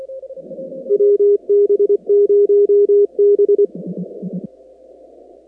Morse code, heard as a keyed tone, about 24 wpm, 410 Hz. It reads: WB0B
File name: contests / 2019-cw-wpx-kt5j-k5pi- -2330 0-44